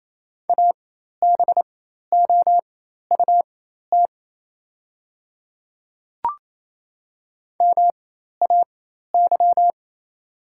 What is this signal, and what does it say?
Morse code, heard as a keyed tone, about 28 wpm, spaced as for 13 wpm, 700 Hz. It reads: ABOUT MAY
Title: 4x Character Spacing - Top 100 Words - 28wpm